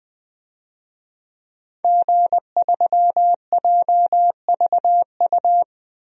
Morse code, heard as a keyed tone, about 21 wpm, 700 Hz. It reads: G3JVU
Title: DXCC-100 Call Signs - 5 Characters - Repeated 2x - 20wpm